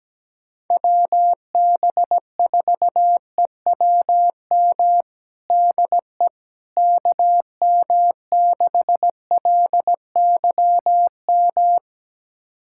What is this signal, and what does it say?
Morse code, heard as a keyed tone, about 17 wpm, 700 Hz. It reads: WB4EWM DE KM6LYM
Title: USA Callsign DE Callsign 17wpm